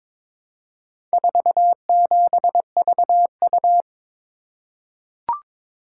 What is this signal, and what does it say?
Morse code, heard as a keyed tone, about 22 wpm, 700 Hz. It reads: 47VU
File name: practice / 22wpm / Four Letter-Number Combinations - 22wpm